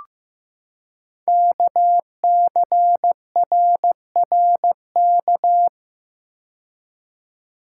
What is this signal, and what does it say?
Morse code, heard as a keyed tone, about 15 wpm, 700 Hz. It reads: KCRRK
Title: Code Groups - 5 Letter:Numbers - 15wpm